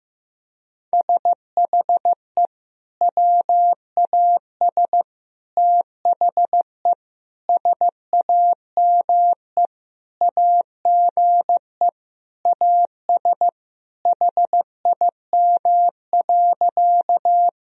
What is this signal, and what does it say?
Morse code, heard as a keyed tone, about 15 wpm, 700 Hz. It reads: SHE WAS THE SAME AGE AS HIM.
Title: Sentences from Top 1000 Words - Review Part 3 - 15wpm